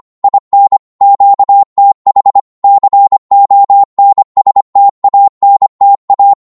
Morse code, heard as a keyed tone, about 25 wpm, 800 Hz. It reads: IN QTH CONSTANTA
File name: Morse / qso-25wpm